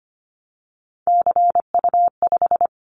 Morse code, heard as a keyed tone, about 25 wpm, 700 Hz. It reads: CU5
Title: Callsign Prefixes for Top 50 DXCC - 25wpm